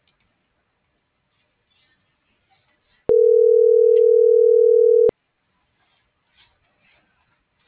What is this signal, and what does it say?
Ringback tone (North America), 440 + 480 Hz.